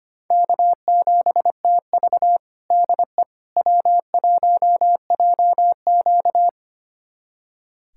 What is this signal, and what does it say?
Morse code, heard as a keyed tone, about 25 wpm, 700 Hz. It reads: K7TV DE W1JQ
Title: USA Callsign DE Callsign - 4 Characters - 25wpm